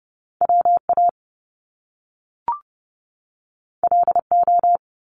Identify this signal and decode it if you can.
Morse code, heard as a keyed tone, about 30 wpm, 700 Hz. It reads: WA LO
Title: Top Two Letter Combinations in Words 30wpm